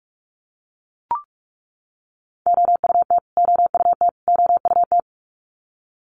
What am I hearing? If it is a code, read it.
Morse code, heard as a keyed tone, about 44 wpm, 700 Hz. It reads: OUT OUT OUT
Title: Top 200 Words Repeated 45wpm